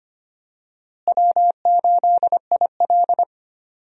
Morse code, heard as a keyed tone, about 25 wpm, 700 Hz. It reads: W8IL
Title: USA Call Signs - 4 Characters - Repeated 2x - 25wpm